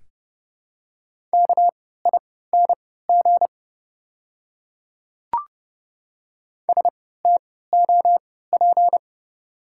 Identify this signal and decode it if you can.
Morse code, heard as a keyed tone, about 30 wpm, spaced as for 17 wpm, 690 Hz. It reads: KING STOP
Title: 3x Character Spacing - Top 500 Words - 30wpm